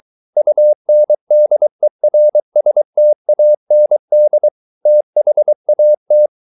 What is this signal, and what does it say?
Morse code, heard as a keyed tone, about 23 wpm, 600 Hz. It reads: UNDERSTAND THAT